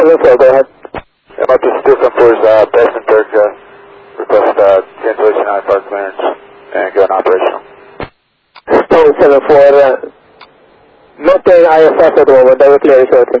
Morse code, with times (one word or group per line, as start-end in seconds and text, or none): none